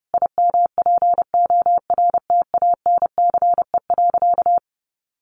Morse code, heard as a keyed tone, about 30 wpm, 700 Hz. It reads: IMPORTANCE.
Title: Sentences from Top 1200 Words 30wpm